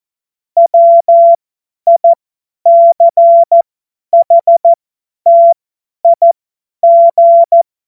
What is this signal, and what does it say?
Morse code, heard as a keyed tone, about 14 wpm, spaced as for 10 wpm, 680 Hz. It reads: WICHTIG
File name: Morsekurs / Witze / Witz 23. Dezember 2025 50 F70